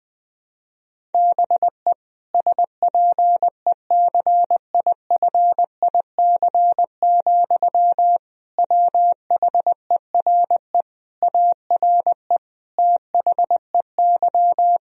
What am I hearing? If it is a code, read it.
Morse code, heard as a keyed tone, about 20 wpm, 700 Hz. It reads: BE SPECIFIC, WHERE ARE THEY
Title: Sentences from Top 1000 Words - Review Part 3 - 20wpm